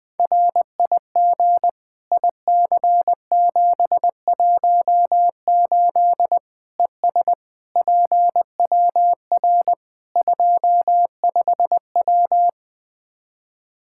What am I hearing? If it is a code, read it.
Morse code, heard as a keyed tone, about 20 wpm, 700 Hz. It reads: RIG IC718 ES PWR 25W
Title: Practice QSO - Exchange 2 - Info - 20wpm